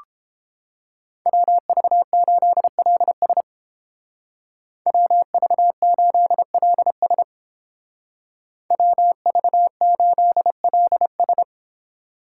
Morse code, heard as a keyed tone, about 30 wpm, 700 Hz. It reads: WV8LS WV8LS WV8LS